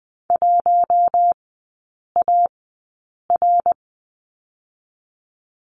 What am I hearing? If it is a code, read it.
Morse code, heard as a keyed tone, about 20 wpm, spaced as for 8 wpm, 700 Hz. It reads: 1AR
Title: 060 Lesson 11e - Letter h - Three Characters 20wpm